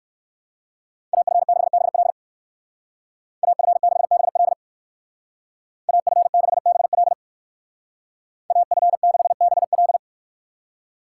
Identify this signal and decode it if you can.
Morse code, heard as a keyed tone, about 65 wpm, 700 Hz. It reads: AF6BL AF6BL AF6BL AF6BL